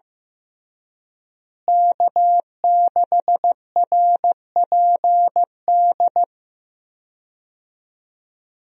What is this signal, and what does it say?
Morse code, heard as a keyed tone, about 16 wpm, 700 Hz. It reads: K6RPD
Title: USA Call Signs - 5 Characters - Repeated 2x - 15wpm